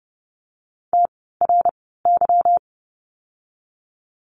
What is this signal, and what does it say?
Morse code, heard as a keyed tone, about 30 wpm, spaced as for 17 wpm, 700 Hz. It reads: TRY
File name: course / 30:10wpm / 230 Lesson 37f - Letter x - Words 30wpm